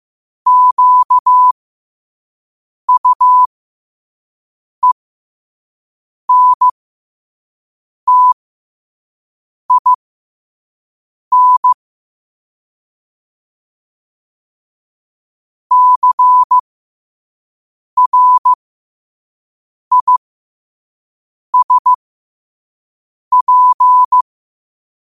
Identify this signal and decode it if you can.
Morse code, heard as a keyed tone, about 15 wpm, spaced as for 5 wpm, 1000 Hz. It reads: QUENTIN CRISP
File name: qotdfiles / 5wpm